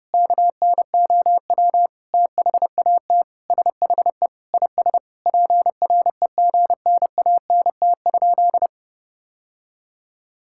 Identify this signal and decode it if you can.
Morse code, heard as a keyed tone, about 32 wpm, 700 Hz. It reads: KNOW THAT SHE IS PREGNANT?
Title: Sentences from Top 1000 Words - Review Part 6 - 30wpm